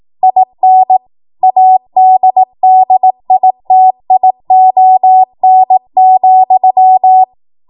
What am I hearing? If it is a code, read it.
Morse code, heard as a keyed tone, about 18 wpm, 750 Hz. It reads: IN ADDITION,
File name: Morse / 18WPM